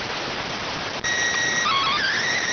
SSTV (slow-scan television), PD 290.